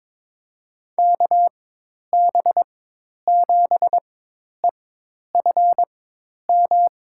Morse code, heard as a keyed tone, about 22 wpm, spaced as for 10 wpm, 700 Hz. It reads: KB7EFM
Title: USA Call Signs - 6 Characters - 4x Character Spacing - 22wpm